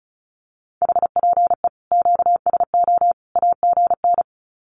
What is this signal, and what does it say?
Morse code, heard as a keyed tone, about 34 wpm, 700 Hz. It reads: HPE QSO AGN
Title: Common QSO Phrases 35wpm